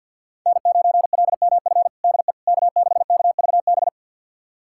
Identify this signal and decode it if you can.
Morse code, heard as a keyed tone, about 49 wpm, 700 Hz. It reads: N9LMF DE K6KVB
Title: USA Callsign DE Callsign - 5 Characters - 50wpm